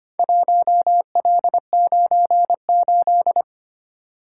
Morse code, heard as a keyed tone, about 25 wpm, 700 Hz. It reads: 1L98
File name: practice / 25wpm / Four Letter-Number Combinations - 25wpm